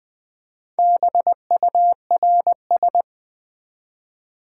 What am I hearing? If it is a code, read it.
Morse code, heard as a keyed tone, about 20 wpm, 700 Hz. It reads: BURS
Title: Build A Word - 1 Letter At A Time - 5-Letter Words - 20wpm